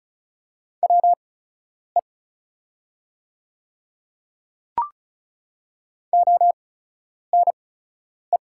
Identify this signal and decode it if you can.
Morse code, heard as a keyed tone, about 33 wpm, spaced as for 10 wpm, 690 Hz. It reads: WE ONE
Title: 8x Character Spacing - Top 100 Words - 35wpm